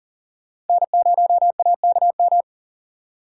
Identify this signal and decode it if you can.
Morse code, heard as a keyed tone, about 50 wpm, 700 Hz. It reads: N0AKM